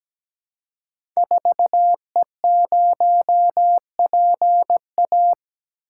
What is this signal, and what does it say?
Morse code, heard as a keyed tone, about 17 wpm, 700 Hz. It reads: 4E0PA
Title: Code Groups - 5 Letter:Numbers - 17wpm